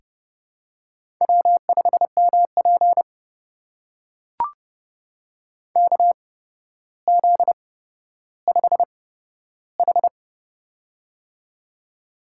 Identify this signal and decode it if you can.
Morse code, heard as a keyed tone, about 30 wpm, spaced as for 13 wpm, 700 Hz. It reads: W5MP KZ5H
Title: USA Call Signs - 4 Characters - 8x Character Spacing - 30wpm